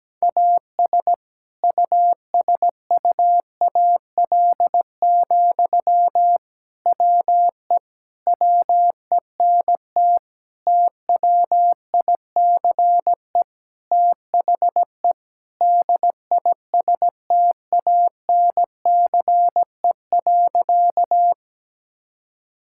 Morse code, heard as a keyed tone, about 17 wpm, 700 Hz. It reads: AS USUAL, WE WENT TWICE THE DISTANCE.
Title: Sentences from Top 1600 Words 17wpm